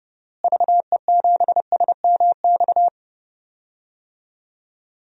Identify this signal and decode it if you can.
Morse code, heard as a keyed tone, about 30 wpm, 700 Hz. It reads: VE7SMX
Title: DXCC-100 Call Signs - 6 Characters - Repeated 2x - 30wpm